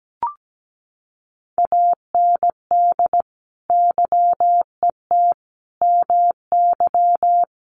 Morse code, heard as a keyed tone, about 17 wpm, 700 Hz. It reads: AND YET MY
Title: Sentences from Top 1200 Words 17wpm